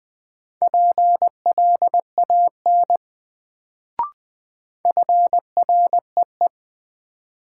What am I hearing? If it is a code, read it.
Morse code, heard as a keyed tone, about 20 wpm, 700 Hz. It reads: PLAN FREE
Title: Sentences from Top 500 Words - New Word Drill - 20wpm